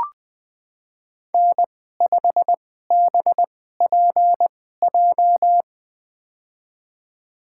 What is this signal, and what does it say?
Morse code, heard as a keyed tone, about 20 wpm, spaced as for 15 wpm, 700 Hz. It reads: N5BPJ
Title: USA Call Signs - 2x Character Spacing - 20wpm